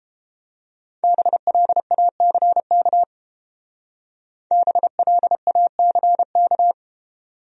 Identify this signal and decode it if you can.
Morse code, heard as a keyed tone, about 28 wpm, 700 Hz. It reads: BLACK BLACK